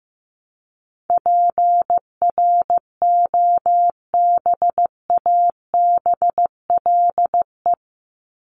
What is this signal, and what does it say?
Morse code, heard as a keyed tone, about 15 wpm, 700 Hz. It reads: PROBABLE
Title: LY Suffix - Top 100 Words with Base Words 15wpm